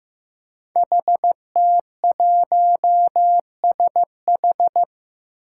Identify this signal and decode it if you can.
Morse code, heard as a keyed tone, about 15 wpm, 700 Hz. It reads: HT1SH
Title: Five Challenging Letter-Number Combinations 15wpm